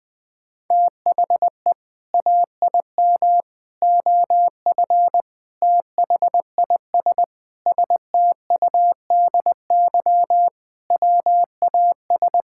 Morse code, heard as a keyed tone, about 20 wpm, 700 Hz. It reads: THE AIM OF THIS STUDY WAS
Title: Sentences with 7-Word Phrases - Spoken Beginning - 20wpm